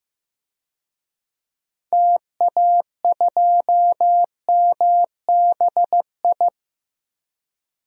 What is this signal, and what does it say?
Morse code, heard as a keyed tone, about 15 wpm, 700 Hz. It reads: TA2MBI
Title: DXCC-100 Call Signs - 6 Characters - 15wpm